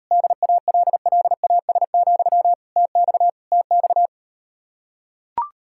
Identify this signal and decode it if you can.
Morse code, heard as a keyed tone, about 39 wpm, 700 Hz. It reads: DALLAS, TX TX